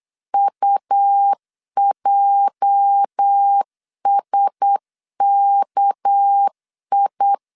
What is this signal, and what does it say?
Morse code, heard as a keyed tone, about 8 wpm, 800 Hz. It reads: UJSKI
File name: LEZIONI / 8 YJ45 / 8 YJ 8wpm 3313